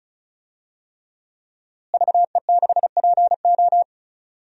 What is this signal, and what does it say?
Morse code, heard as a keyed tone, about 34 wpm, 700 Hz. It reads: VE6PO
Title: DXCC-100 Call Signs - 5 Characters - Repeated 3x - 35wpm